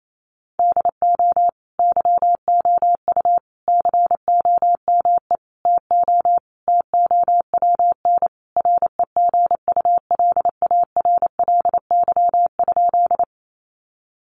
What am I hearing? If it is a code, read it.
Morse code, heard as a keyed tone, about 28 wpm, 700 Hz. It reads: DO YOU COME TO TOWN REGULARLY?
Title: Sentences from Top 1400 Words 28wpm